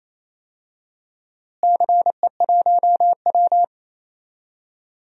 Morse code, heard as a keyed tone, about 28 wpm, 700 Hz. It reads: CE1W
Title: DXCC-100 Call Signs - 4 Characters - Repeated 2x - 28wpm